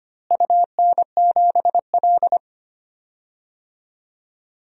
Morse code, heard as a keyed tone, about 25 wpm, 700 Hz. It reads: UN7L
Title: DXCC-100 Call Signs - 4 Characters - Repeated 2x - 25wpm